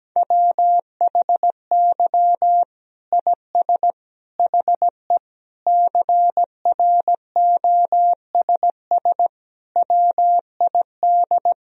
Morse code, heard as a keyed tone, about 17 wpm, 700 Hz. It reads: WHY IS HE CROSS WID
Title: Sentences from Top 300 Words and Abbreviations 17wpm